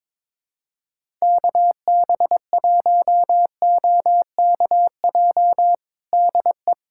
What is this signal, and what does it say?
Morse code, heard as a keyed tone, about 22 wpm, 700 Hz. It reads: KB1OKJ DE
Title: USA Callsign DE Callsign 22wpm